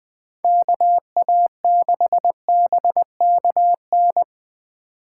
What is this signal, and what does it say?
Morse code, heard as a keyed tone, about 20 wpm, 700 Hz. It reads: KA6BKN
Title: USA Call Signs - 6 Characters - 2x Character Spacing - 20wpm